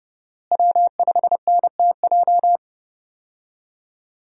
Morse code, heard as a keyed tone, about 31 wpm, 700 Hz. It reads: W5NTJ
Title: USA Call Signs - 5 Characters - Repeated 3x - 30wpm